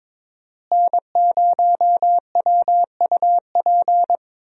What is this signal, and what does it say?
Morse code, heard as a keyed tone, about 22 wpm, 700 Hz. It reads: N0WUP
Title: USA Call Signs - 5 Characters - Repeated 2x - 22wpm